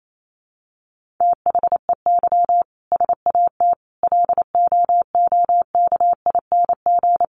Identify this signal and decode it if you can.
Morse code, heard as a keyed tone, about 28 wpm, 700 Hz. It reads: THEY SAT LOOKING
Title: A Sound Of Thunder 28wpm